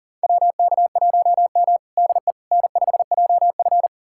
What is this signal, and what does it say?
Morse code, heard as a keyed tone, about 40 wpm, 700 Hz. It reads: WK1M DE N5JF
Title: USA Callsign DE Callsign - 4 Characters - 40wpm